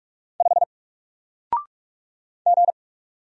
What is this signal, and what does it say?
Morse code, heard as a keyed tone, about 44 wpm, 700 Hz. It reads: 5 G
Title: Single Letter-Number Alt 45wpm